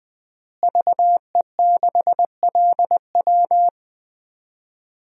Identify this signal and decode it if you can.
Morse code, heard as a keyed tone, about 20 wpm, 700 Hz. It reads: VE6LW
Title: DXCC-100 Call Signs - 5 Characters - Repeated 3x - 20wpm